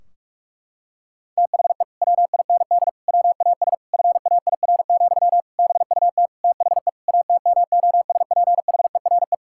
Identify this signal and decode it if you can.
Morse code, heard as a keyed tone, about 42 wpm, 700 Hz. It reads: THE WIND WAS FAIR, BUT THE ATMOSPHERE
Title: Sentences from Top 1600 Words 45wpm